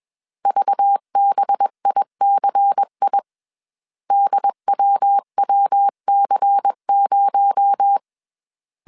Morse code, heard as a keyed tone, about 21 wpm, 800 Hz. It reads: VBICI DWWC0
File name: CW / DET / 1 6 1500C S C 20wpm 2116